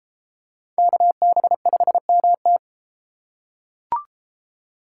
Morse code, heard as a keyed tone, about 30 wpm, 700 Hz. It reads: KB5MT